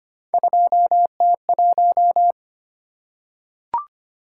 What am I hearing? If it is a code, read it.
Morse code, heard as a keyed tone, about 25 wpm, 700 Hz. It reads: 2T1
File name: Three Letter-Number Combinations - 25wpm